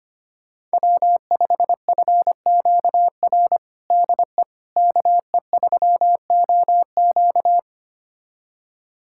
Morse code, heard as a keyed tone, about 25 wpm, 700 Hz. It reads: W5FQR DE KE3OQ
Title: USA Callsign DE Callsign - 5 Characters - 25wpm